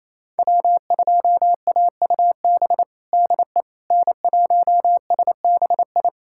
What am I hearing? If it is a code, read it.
Morse code, heard as a keyed tone, about 28 wpm, 700 Hz. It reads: W2AUB DE N1SBI
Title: USA Callsign DE Callsign 28wpm